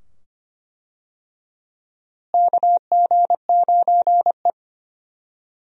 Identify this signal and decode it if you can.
Morse code, heard as a keyed tone, about 25 wpm, 700 Hz. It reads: KG9E
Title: USA Call Signs - 4 Characters - Repeated 2x - 25wpm